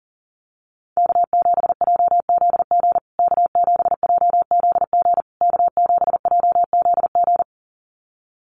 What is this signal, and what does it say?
Morse code, heard as a keyed tone, about 40 wpm, 700 Hz. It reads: K7JZG K7JZG K7JZG
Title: USA Call Signs Repeated 3x 40wpm